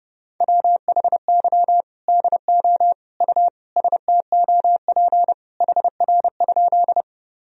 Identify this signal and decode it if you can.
Morse code, heard as a keyed tone, about 30 wpm, 700 Hz. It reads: WHY DO U STOP HR?
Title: Sentences from Top 300 Words and Abbreviations 30wpm